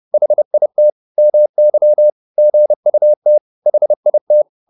Morse code, heard as a keyed tone, about 30 wpm, 600 Hz. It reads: HIT MY GUT HIT